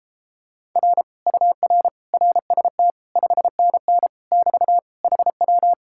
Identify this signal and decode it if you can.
Morse code, heard as a keyed tone, about 29 wpm, 700 Hz. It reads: R UR RST 5NN <BT> HW